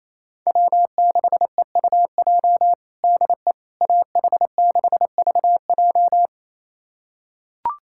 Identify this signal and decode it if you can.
Morse code, heard as a keyed tone, about 28 wpm, 700 Hz. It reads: W6EUJ DE AH6VJ
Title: USA Callsign DE Callsign - 5 Characters - 28wpm